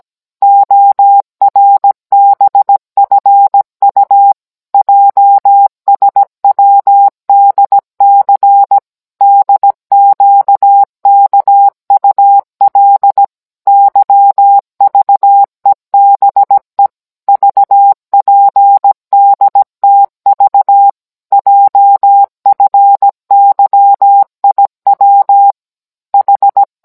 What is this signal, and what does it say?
Morse code, heard as a keyed tone, about 17 wpm, 800 Hz. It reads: ORBFU JSWDC DQKUL YVEBE VPDTV JFYIW H